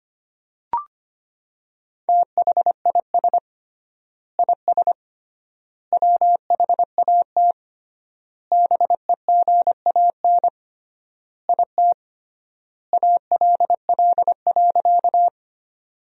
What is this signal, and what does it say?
Morse code, heard as a keyed tone, about 25 wpm, 700 Hz. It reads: THIS IS WHAT BEGAN IT ALL.
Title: 3x Word-Spacing - Sentences from Top 300 Words - 25wpm